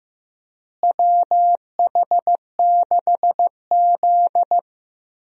Morse code, heard as a keyed tone, about 15 wpm, 700 Hz. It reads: WH6Z
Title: USA Call Signs - 4 Characters - Repeated 2x - 15wpm